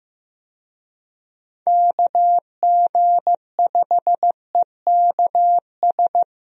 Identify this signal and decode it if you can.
Morse code, heard as a keyed tone, about 16 wpm, 700 Hz. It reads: KG5EKS
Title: USA Call Signs - 6 Characters - Repeated 2x - 15wpm